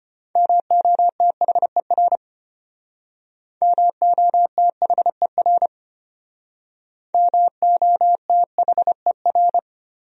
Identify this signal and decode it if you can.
Morse code, heard as a keyed tone, about 31 wpm, 700 Hz. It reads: MOTHER MOTHER MOTHER
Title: Speed-Racing Top 200 Words 25wpm